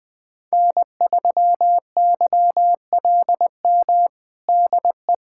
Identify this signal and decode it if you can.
Morse code, heard as a keyed tone, about 20 wpm, 700 Hz. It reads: N3YLM DE
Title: USA Callsign DE Callsign - 5 Characters - 20wpm